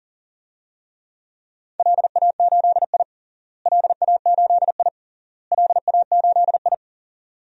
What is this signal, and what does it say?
Morse code, heard as a keyed tone, about 40 wpm, 700 Hz. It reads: LA8I LA8I LA8I
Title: DXCC-100 Call Signs - 4 Characters - Repeated 3x - 40wpm